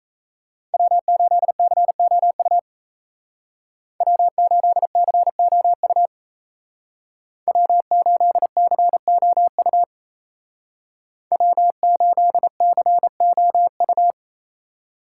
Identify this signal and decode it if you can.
Morse code, heard as a keyed tone, about 36 wpm, 700 Hz. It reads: W8COU W8COU W8COU W8COU